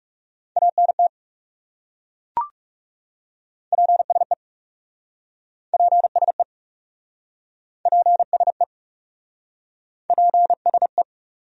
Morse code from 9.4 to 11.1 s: PSE

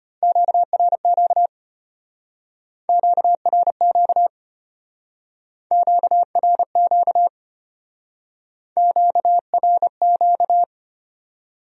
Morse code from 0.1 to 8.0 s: QRQ QRQ QRQ

